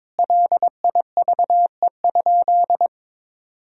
Morse code, heard as a keyed tone, about 21 wpm, 700 Hz. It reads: LIVE?
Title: Sentences from Top 200 Words and Abbreviations 22wpm